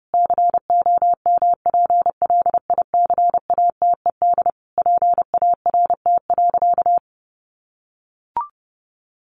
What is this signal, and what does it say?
Morse code, heard as a keyed tone, about 28 wpm, 700 Hz. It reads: COMPLICATED PART.
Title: Sentences from Top 1200 Words 30wpm